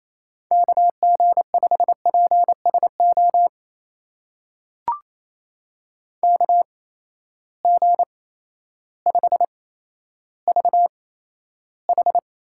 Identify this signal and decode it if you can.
Morse code, heard as a keyed tone, about 28 wpm, spaced as for 13 wpm, 700 Hz. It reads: KG5PSO KG5VH
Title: USA Call Signs - 6 Characters - 8x Character Spacing - 28wpm